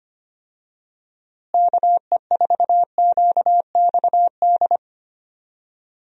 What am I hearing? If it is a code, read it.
Morse code, heard as a keyed tone, about 25 wpm, 700 Hz. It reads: KE4QXD
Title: USA Call Signs - 6 Characters - Repeated 3x - 25wpm